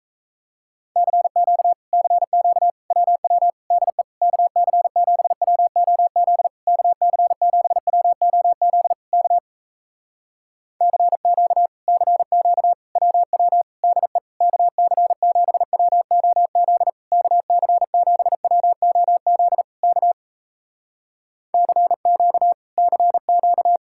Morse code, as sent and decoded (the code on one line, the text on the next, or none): -.-. --.- / -.-. --.- / .-- .-- / -.. / -.- -.-. --... .-- --- --.. / -.- -.-. --... .-- --- --.. / -.- / -.-. --.- / -.-. --.- / .-- .-- / -.. / -.- -.-. --... .-- --- --.. / -.- -.-. --... .-- --- --.. / -.- / -.-. --.- / -.-. --.-
CQ CQ WW DE KC7WOZ KC7WOZ K CQ CQ WW DE KC7WOZ KC7WOZ K CQ CQ